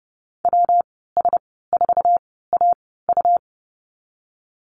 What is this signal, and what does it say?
Morse code, heard as a keyed tone, about 30 wpm, 700 Hz. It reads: W S 4 A U
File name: Lesson 19g - Letter w - Call Signs 30wpm